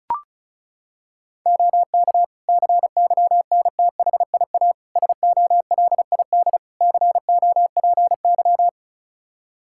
Morse code, heard as a keyed tone, about 36 wpm, 700 Hz. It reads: OK CYNTHIA SOLID COPY
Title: K1USN SST QSO Practice 35wpm